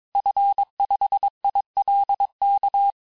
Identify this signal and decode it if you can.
Morse code, heard as a keyed tone, about 22 wpm, 800 Hz. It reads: F5ILK
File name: cours-du-mercredi / audios / cours-854-liste-fin-QSO-vitesse-21-mpm